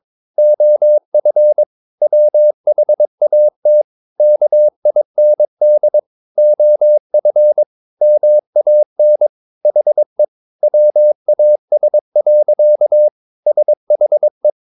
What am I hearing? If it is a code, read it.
Morse code, heard as a keyed tone, about 22 wpm, 600 Hz. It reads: OF WHAT KIND OF MAN HE WAS. SHE
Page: Romanen Never af Ken Follet på engelsk.